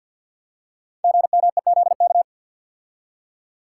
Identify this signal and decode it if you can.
Morse code, heard as a keyed tone, about 50 wpm, 700 Hz. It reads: GMEZK
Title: Code Groups - 5 Letter:Numbers - 50wpm